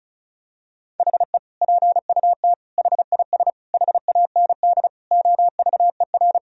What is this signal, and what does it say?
Morse code, heard as a keyed tone, about 34 wpm, 700 Hz. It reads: HE PUT HIS HAND OVER
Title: Sentences from Top 1000 Words - Review Part 2 - 35wpm